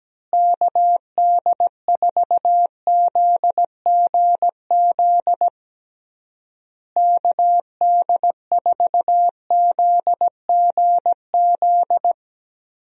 Morse code, heard as a keyed tone, about 17 wpm, 700 Hz. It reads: KD4ZGZ KD4ZGZ
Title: USA Call Signs - 6 Characters - Repeated 3x - 17wpm